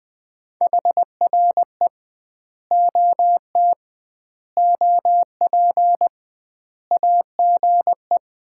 Morse code, heard as a keyed tone, about 21 wpm, 700 Hz. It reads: HRE OT OP AGE
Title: Common QSO Phrases - 2x Word-Spacing - 20wpm